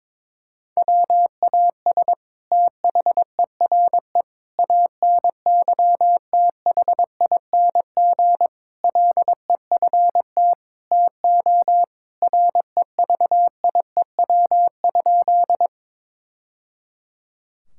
Morse code, heard as a keyed tone, about 22 wpm, 700 Hz. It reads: WAS THERE ANYTHING LEFT TO REVIEW?
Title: Sentences from Top 1000 Words - Review Part 6 - 22wpm